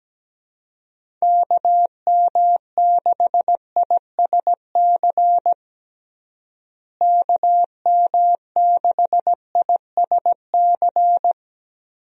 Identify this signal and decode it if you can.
Morse code, heard as a keyed tone, about 17 wpm, 700 Hz. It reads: KM6ISC KM6ISC
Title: USA Call Signs - 6 Characters - Repeated 3x - 17wpm